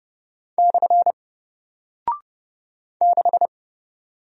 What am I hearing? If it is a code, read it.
Morse code, heard as a keyed tone, about 30 wpm, 700 Hz. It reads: / 6
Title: Single Character (Letter:Number:Punctuation:Prosign) Alt 30wpm